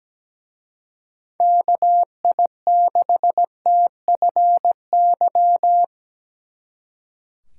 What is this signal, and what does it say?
Morse code, heard as a keyed tone, about 20 wpm, 700 Hz. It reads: KI6TFY
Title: Speed Racing - USA 2x3 Call Signs 17wpm